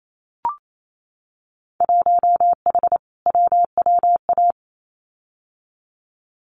Morse code, heard as a keyed tone, about 28 wpm, 700 Hz. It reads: 1H WWA
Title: Winter Field Day Exchanges 28wpm